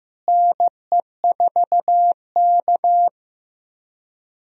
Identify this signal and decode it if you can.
Morse code, heard as a keyed tone, about 15 wpm, 700 Hz. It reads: NE4K
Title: USA Call Signs - 4 Characters - Repeated 3x - 15wpm